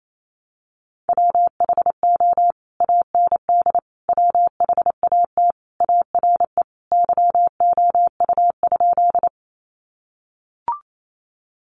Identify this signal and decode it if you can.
Morse code, heard as a keyed tone, about 28 wpm, 700 Hz. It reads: WHO AND WHAT ARE YOU?
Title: Sentences from Top 100 Words 28wpm